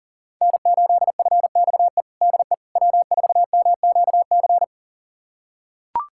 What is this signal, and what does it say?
Morse code, heard as a keyed tone, about 40 wpm, 700 Hz. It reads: N8FXE DE W4MQC
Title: USA Callsign DE Callsign 40wpm